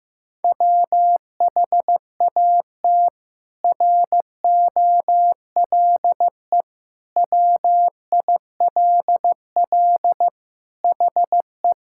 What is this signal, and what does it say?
Morse code, heard as a keyed tone, about 15 wpm, 700 Hz. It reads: WHAT ROLE WILL HE
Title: Sentences from Top 1000 Words - Review Part 6 - 15wpm